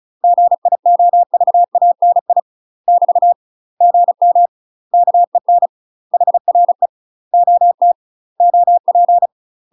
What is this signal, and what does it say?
Morse code, heard as a keyed tone, about 34 wpm, 700 Hz. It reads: GIOVANI <BT> GM KEN HRE OT OP
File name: UK MASTER QSOs TRAINING / 35WPM / UK MASTER QSOs 0018